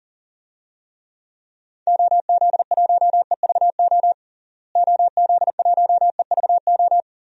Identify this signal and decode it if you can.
Morse code, heard as a keyed tone, about 40 wpm, 700 Hz. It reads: OZ1EVO OZ1EVO
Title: DXCC-100 Call Signs - 6 Characters - Repeated 2x - 40wpm